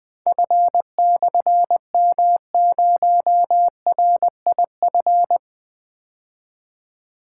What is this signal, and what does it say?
Morse code, heard as a keyed tone, about 20 wpm, 700 Hz. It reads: F/M0RIF
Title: Difficult Call Signs 20wpm